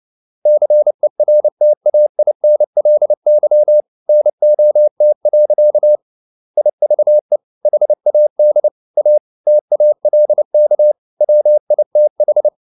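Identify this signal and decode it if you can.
Morse code, heard as a keyed tone, about 29 wpm, 600 Hz. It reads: CERTAINLY NOT. IVE HAD A TALK WITH
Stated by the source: Morse code at 29 wpm, sending Ken Follett's novel Never in English, from CW med Gnister.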